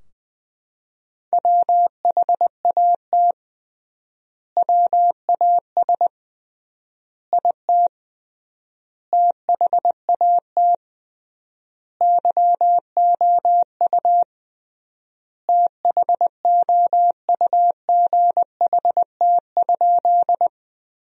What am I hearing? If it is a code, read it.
Morse code, heard as a keyed tone, about 20 wpm, 700 Hz. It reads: WHAT WAS IT THAT YOU THOUGHT?
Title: 3x Word-Spacing - Sentences from Top 300 Words - 20wpm